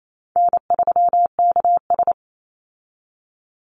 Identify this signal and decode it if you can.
Morse code, heard as a keyed tone, about 28 wpm, 700 Hz. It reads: N3KS
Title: Easy Call Signs Repeated 3x - 28wpm